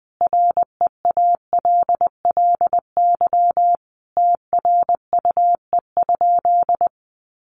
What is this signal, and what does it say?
Morse code, heard as a keyed tone, about 20 wpm, 700 Hz. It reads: REALLY TRUE?
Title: Sentences from Top 600 Words 20wpm